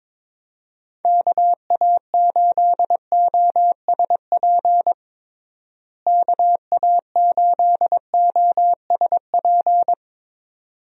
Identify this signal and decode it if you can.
Morse code, heard as a keyed tone, about 22 wpm, 700 Hz. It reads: KA8OSP KA8OSP
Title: USA Call Signs - 6 Characters - Repeated 2x - 22wpm